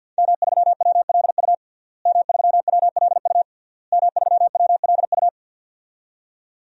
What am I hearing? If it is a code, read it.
Morse code, heard as a keyed tone, about 52 wpm, 700 Hz. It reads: M3WLU M3WLU M3WLU
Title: DXCC-100 Call Signs - 5 Characters - Repeated 3x - 50wpm